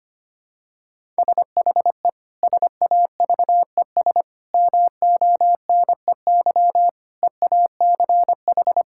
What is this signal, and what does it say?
Morse code, heard as a keyed tone, about 25 wpm, 700 Hz. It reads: SHE SAVES MONEY EACH
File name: practice / 25wpm / Difficult Sentences with Hints - 25wpm